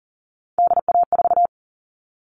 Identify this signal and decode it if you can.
Morse code, heard as a keyed tone, about 40 wpm, 700 Hz. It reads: DA4
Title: All Cut Numbers - 3-Digit 40wpm